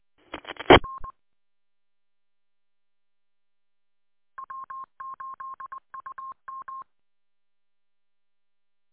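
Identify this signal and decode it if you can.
Morse code, heard as a keyed tone, about 20 wpm, 1070 Hz. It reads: R W8UM
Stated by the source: Morse code from the W8UM audio archive on 145.230.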